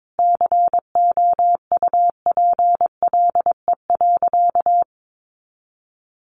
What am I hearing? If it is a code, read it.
Morse code, heard as a keyed tone, about 22 wpm, 700 Hz. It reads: COUPLE.
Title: Sentences from Top 1100 Words 22wpm